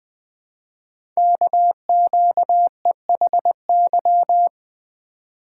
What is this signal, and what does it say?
Morse code, heard as a keyed tone, about 20 wpm, 700 Hz. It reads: KQEHY
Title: Code Groups - 5 Letters - 20wpm